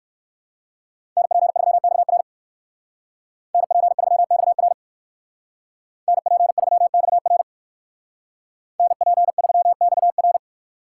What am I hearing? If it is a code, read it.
Morse code, heard as a keyed tone, about 68 wpm, 700 Hz. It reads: NP3XR NP3XR NP3XR NP3XR